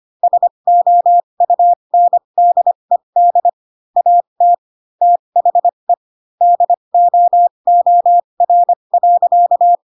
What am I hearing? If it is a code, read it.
Morse code, heard as a keyed tone, about 25 wpm, spaced as for 22 wpm, 700 Hz. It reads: SOUNDED AT THE DOOR.
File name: Pinocchio / Pinocchio ch6 18 25 ik1hge